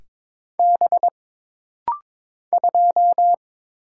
Morse code, heard as a keyed tone, about 22 wpm, 700 Hz. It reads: B 2
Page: This rapid-fire timing is geared to push your ICR (Instant Character Recognition) skills.